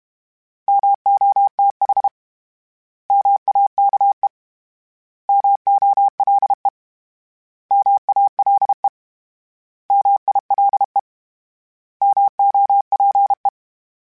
Morse code, heard as a keyed tone, about 32 wpm, 800 Hz. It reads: MOTH MAKE MOLE MALE MILE MOPE
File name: Misc / Misc-mp3s / Misc @30